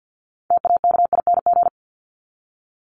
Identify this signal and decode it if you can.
Morse code, heard as a keyed tone, about 48 wpm, 700 Hz. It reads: TAKING
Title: ING Suffix - Top 100 Words with Base Words 50wpm